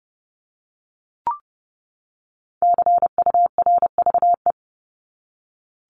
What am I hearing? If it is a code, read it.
Morse code, heard as a keyed tone, about 30 wpm, 700 Hz. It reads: CURVE
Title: Top 1501-1750 Words 30wpm